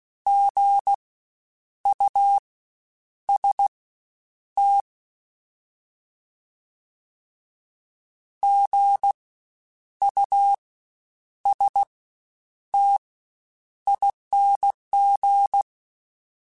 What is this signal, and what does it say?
Morse code, heard as a keyed tone, about 16 wpm, spaced as for 8 wpm, 800 Hz. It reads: GUST GUSTING
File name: Spaced Alphas / Spaced Alphas mp3s / Gs sp @15